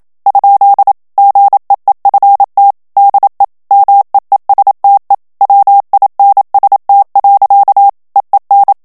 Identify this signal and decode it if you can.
Morse code, heard as a keyed tone, about 27 wpm, 800 Hz. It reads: ? GEEFT DE MEESTE WINST. EEN
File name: HOLLANDAIS texte clair 1 vitesse 26 mots minute